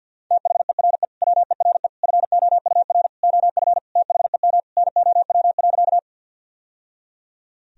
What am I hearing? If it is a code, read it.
Morse code, heard as a keyed tone, about 50 wpm, 700 Hz. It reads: THERE WERE FOUR OF THEM NOW.
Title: Sentences from Top 1000 Words - Review Part 1 - 50wpm